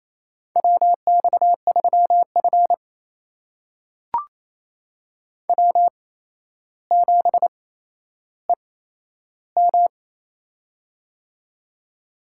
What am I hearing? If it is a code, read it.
Morse code, heard as a keyed tone, about 28 wpm, spaced as for 12 wpm, 700 Hz. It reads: WX3F W7EM
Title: USA Call Signs - 8x Character Spacing - 28wpm